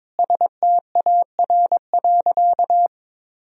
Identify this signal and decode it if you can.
Morse code, heard as a keyed tone, about 22 wpm, 700 Hz. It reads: STAR.
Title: Sentences from Top 1000 Words - Review Part 3 - 22wpm